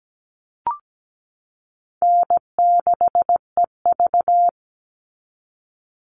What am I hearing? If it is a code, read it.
Morse code, heard as a keyed tone, about 17 wpm, 700 Hz. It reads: N6EV
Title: Easy Call Signs - 17wpm